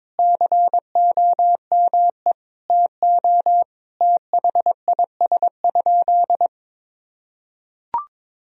Morse code, heard as a keyed tone, about 22 wpm, 700 Hz. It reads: COME TO THIS?
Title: Sentences from Top 100 Words and Abbreviations 22wpm